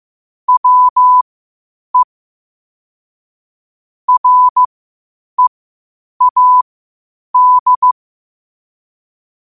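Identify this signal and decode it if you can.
Morse code, heard as a keyed tone, about 15 wpm, spaced as for 8 wpm, 1000 Hz. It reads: WE READ
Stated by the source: Morse code quotes at 8 words per minute from QOTD at 8 WPM.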